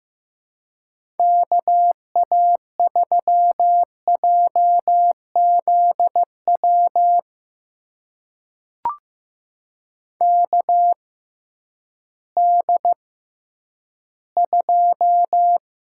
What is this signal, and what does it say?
Morse code, heard as a keyed tone, about 15 wpm, spaced as for 10 wpm, 700 Hz. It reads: KA3JZW KD2
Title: USA Call Signs - 6 Characters - 6x Character Spacing - 15wpm